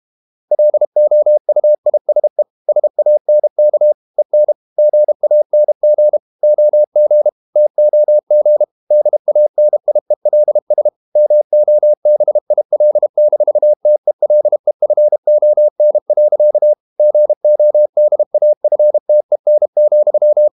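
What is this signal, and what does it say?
Morse code, heard as a keyed tone, about 32 wpm, 600 Hz. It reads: LOUISE SANK EN GANG OG TOG DANIELS MOBIL-TELEFON. GODAFTEN,
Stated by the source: High-speed QRQ CW at 32 WPM.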